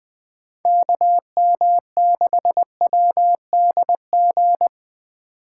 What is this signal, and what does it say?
Morse code, heard as a keyed tone, about 20 wpm, 700 Hz. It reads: KM6WDG
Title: USA Call Signs - 6 Characters - 3x Character Spacing - 20wpm